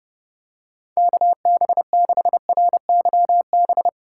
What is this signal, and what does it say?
Morse code, heard as a keyed tone, about 32 wpm, 700 Hz. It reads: KB6RYB
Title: USA Call Signs - 6 Characters - 4x Character Spacing - 30wpm